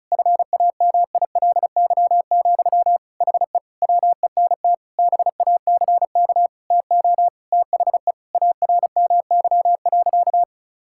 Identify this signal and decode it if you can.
Morse code, heard as a keyed tone, about 36 wpm, 700 Hz. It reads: FAMILY, HE WENT BACK TO THE ARMY.
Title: Sentences from Top 1000 Words - Review Part 6 - 35wpm